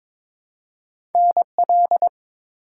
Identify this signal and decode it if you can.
Morse code, heard as a keyed tone, about 22 wpm, 700 Hz. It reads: NL
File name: ARRL Sections v2 - 22wpm